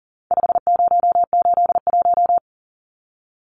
Morse code, heard as a keyed tone, about 39 wpm, 700 Hz. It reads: 5081
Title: Numbers - 4-Digit 40wpm